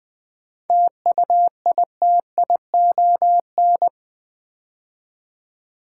Morse code, heard as a keyed tone, about 20 wpm, 700 Hz. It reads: TUITION
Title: 7 Letter Words - First 3 Letters Given - 20wpm